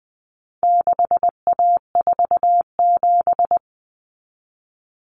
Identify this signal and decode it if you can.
Morse code, heard as a keyed tone, about 20 wpm, 700 Hz. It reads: 6A47
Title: Common Cut Numbers - 4-Digit 20wpm